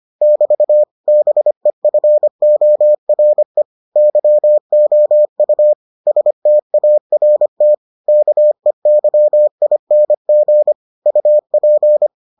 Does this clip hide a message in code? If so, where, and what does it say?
Yes: Morse code from 0.2 to 12.1 s, heard as a keyed tone, about 25 wpm, 600 Hz. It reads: <BT> BEFORE YOU START KEYING UP